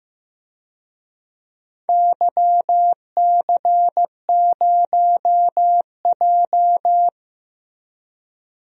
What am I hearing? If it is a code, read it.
Morse code, heard as a keyed tone, about 15 wpm, 700 Hz. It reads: YC0J
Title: DXCC-100 Call Signs - 4 Characters - Repeated 3x - 15wpm